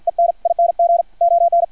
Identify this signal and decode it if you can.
Morse code, heard as a keyed tone, about 56 wpm, 690 Hz, with radio noise. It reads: ENING ON
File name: archived / CRSnet